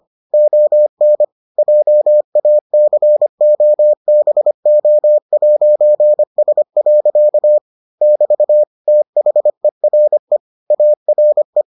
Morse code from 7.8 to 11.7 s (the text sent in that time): <BT> THERE ARE